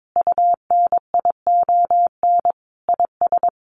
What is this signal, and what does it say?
Morse code, heard as a keyed tone, about 22 wpm, 700 Hz. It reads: UNION IS
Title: Sentences from Top 1400 Words 22wpm